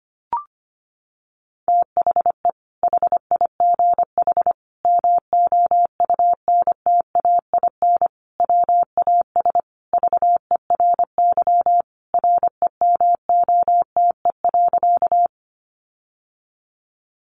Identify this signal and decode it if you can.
Morse code, heard as a keyed tone, about 25 wpm, 700 Hz. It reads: THE HIGH MOUNTAIN WAS VERY REMOTE.
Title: Sentences from Top 1300 Words 25wpm